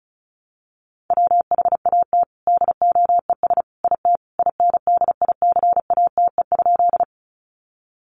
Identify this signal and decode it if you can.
Morse code, heard as a keyed tone, about 34 wpm, 700 Hz. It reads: WHAT DOES IT INDICATE?
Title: Sentences from Top 1300 Words 35wpm